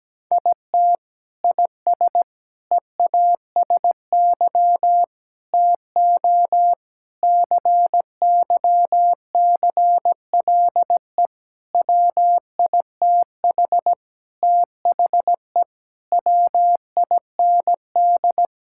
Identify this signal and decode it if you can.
Morse code, heard as a keyed tone, about 17 wpm, 700 Hz. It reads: IT IS EASY TO CYCLE WITH THE WIND
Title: Sentences from Top 1000 Words - Review Part 6 - 17wpm